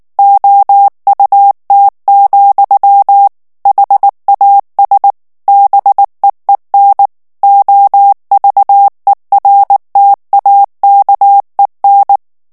Morse code, heard as a keyed tone, about 19 wpm, 800 Hz. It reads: OUT, HAS BEEN OVERTAKEN
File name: ANGLAIS texte clair 3 vitesse 18 mots minute